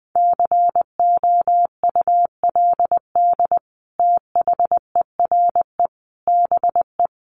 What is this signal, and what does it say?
Morse code, heard as a keyed tone, about 20 wpm, 700 Hz. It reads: COULD THERE BE
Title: Sentences from Top 1500 Words 20wpm